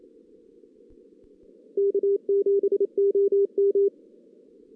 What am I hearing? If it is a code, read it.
Morse code, heard as a keyed tone, about 27 wpm, 400 Hz. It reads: K7OM
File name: 2016 JAN CW NAQP